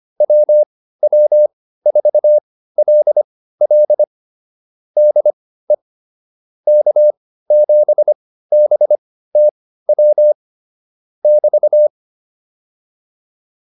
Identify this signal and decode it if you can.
Morse code, heard as a keyed tone, about 25 wpm, spaced as for 15 wpm, 600 Hz. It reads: WW4LL DE K7BTW <BT>
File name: QSO203 15